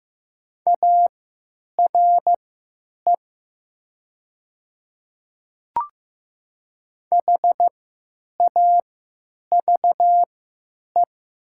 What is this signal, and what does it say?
Morse code, heard as a keyed tone, about 15 wpm, spaced as for 9 wpm, 700 Hz. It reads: ARE HAVE
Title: 3x Character Spacing - Top 100 Words - 15wpm